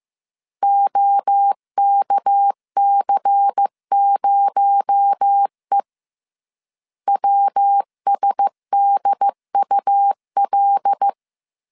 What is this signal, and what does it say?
Morse code, heard as a keyed tone, about 15 wpm, 800 Hz. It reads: OKC0E WSDUL
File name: CW / DET / 1 6 1500C S C 14wpm 2951